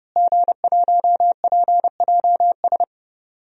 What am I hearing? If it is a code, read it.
Morse code, heard as a keyed tone, about 32 wpm, 700 Hz. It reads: G1PJS
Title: DXCC-100 Call Signs - 5 Characters - Repeated 3x - 30wpm